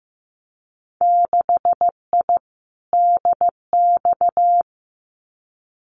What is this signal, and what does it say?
Morse code, heard as a keyed tone, about 16 wpm, 700 Hz. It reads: 6I DX
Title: Winter Field Day Exchanges Encore 15wpm